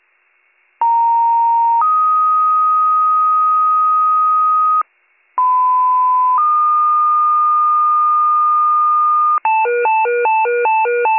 Two-tone page, heard as a one-second tone, then a three-second tone, about 930 then 1250 Hz, about 1010 then 1220 Hz. It is heard from 0.8 to 9.4 s.